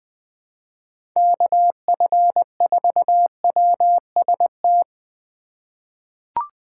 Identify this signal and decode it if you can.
Morse code, heard as a keyed tone, about 20 wpm, 700 Hz. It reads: KF4WST
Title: USA Call Signs - 3x Character Spacing - 20wpm